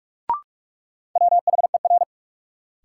Morse code, heard as a keyed tone, about 46 wpm, 700 Hz. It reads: WHER
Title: Sentences from Top 1500 Words - New Word Drill - 45wpm